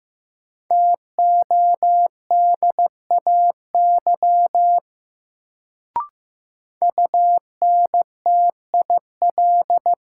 Morse code, heard as a keyed tone, about 15 wpm, 700 Hz. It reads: TODAY UNTIL
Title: Sets of 5 Words Encore - New Word Drill - 15wpm